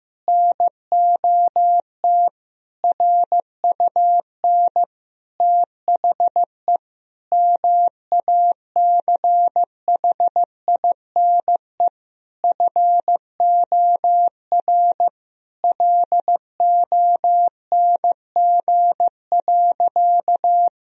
Morse code, heard as a keyed tone, about 16 wpm, 700 Hz. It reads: NOT RUN THE MACHINE FOR LONG.
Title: Sentences from Top 1000 Words - Review Part 2 - 15wpm